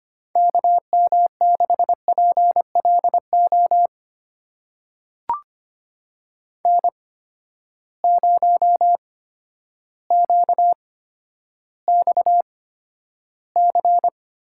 Morse code, heard as a keyed tone, about 25 wpm, spaced as for 11 wpm, 700 Hz. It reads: KM6PLO N0QXC
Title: USA Call Signs - 8x Character Spacing - 25wpm